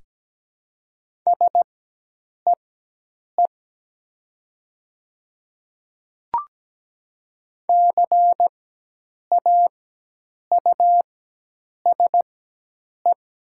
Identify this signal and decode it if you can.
Morse code, heard as a keyed tone, about 17 wpm, spaced as for 8 wpm, 700 Hz. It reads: SEE CAUSE
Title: 4x Character Spacing - Top 200 Words - 17wpm